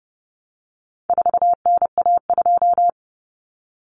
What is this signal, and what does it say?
Morse code, heard as a keyed tone, about 30 wpm, 700 Hz. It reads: 4NA2
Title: Common Cut Numbers - 4-Digit 30wpm